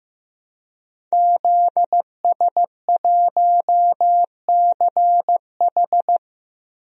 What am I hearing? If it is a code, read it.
Morse code, heard as a keyed tone, about 15 wpm, 700 Hz. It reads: ZS1CH
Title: DXCC-100 Call Signs - 5 Characters - Repeated 3x - 15wpm